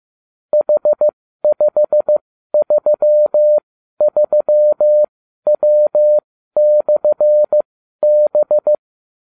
Morse code, heard as a keyed tone, about 15 wpm, spaced as for 12 wpm, 600 Hz. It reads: H533W/B
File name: cw / ubacw30 3